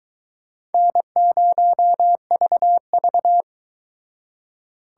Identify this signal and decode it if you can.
Morse code, heard as a keyed tone, about 22 wpm, 700 Hz. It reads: N0VV